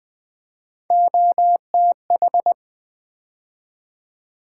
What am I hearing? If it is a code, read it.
Morse code, heard as a keyed tone, about 20 wpm, 700 Hz. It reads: OTH
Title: Build A Word - 3 Letters At A Time - 9-Letter Words - 20wpm